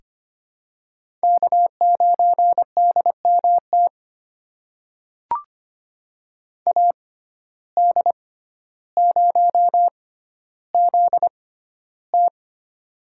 Morse code, heard as a keyed tone, about 25 wpm, spaced as for 13 wpm, 700 Hz. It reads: K9DMT AD0ZT
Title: USA Call Signs - 5 Characters - 6x Character Spacing - 25wpm